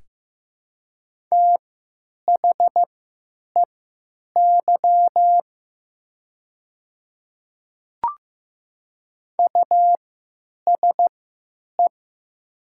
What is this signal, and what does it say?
Morse code, heard as a keyed tone, about 15 wpm, spaced as for 9 wpm, 700 Hz. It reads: THEY USE
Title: 3x Character Spacing - Top 100 Words - 15wpm